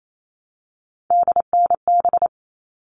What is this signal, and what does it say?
Morse code, heard as a keyed tone, about 28 wpm, 700 Hz. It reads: DNB
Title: All Cut Numbers - 3-Digit 28wpm